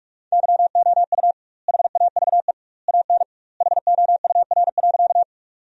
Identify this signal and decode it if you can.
Morse code, heard as a keyed tone, about 44 wpm, 700 Hz. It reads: YOU HAVE AN HOUR.
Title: 2x Word-Spacing - Sentences from Top 400 Words - 45wpm